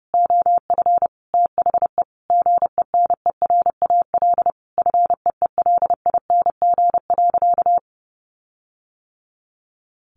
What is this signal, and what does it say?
Morse code, heard as a keyed tone, about 30 wpm, 700 Hz. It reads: OF THE GENERAL FEELING.
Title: Sentences from Top 1400 Words 30wpm